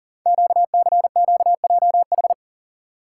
Morse code, heard as a keyed tone, about 38 wpm, 690 Hz. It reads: QCQJH
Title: Code Groups - 5 Letters - 40wpm